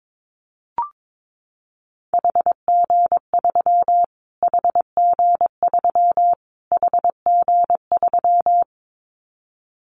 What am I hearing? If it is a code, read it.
Morse code, heard as a keyed tone, about 22 wpm, 700 Hz. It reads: HG3 HG3 HG3
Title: Callsign Prefixes for Top 25 DXCC - Repeated 3x - 22wpm